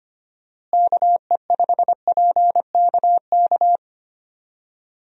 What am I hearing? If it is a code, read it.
Morse code, heard as a keyed tone, about 25 wpm, 700 Hz. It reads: KE5PKK
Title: USA Call Signs - 6 Characters - Repeated 2x - 25wpm